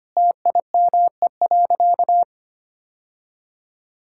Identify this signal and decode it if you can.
Morse code, heard as a keyed tone, about 25 wpm, 700 Hz. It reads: TIME.
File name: practice / 25wpm / Sentences with 3-Word Phrases - 25wpm